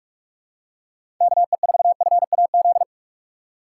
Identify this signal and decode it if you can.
Morse code, heard as a keyed tone, about 46 wpm, 700 Hz. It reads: KE4FAZ
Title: USA Call Signs - 2x Character Spacing - 45wpm